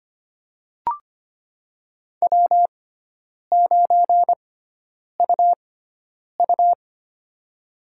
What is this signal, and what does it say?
Morse code, heard as a keyed tone, about 25 wpm, spaced as for 9 wpm, 700 Hz. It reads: W9UU
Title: USA Call Signs - 4 Characters - 6x Character Spacing - 25wpm